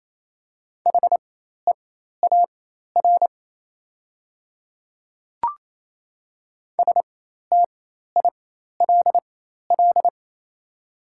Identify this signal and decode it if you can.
Morse code, heard as a keyed tone, about 27 wpm, spaced as for 13 wpm, 700 Hz. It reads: HEAR STILL
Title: 4x Character Spacing - Top 300 Words - 28wpm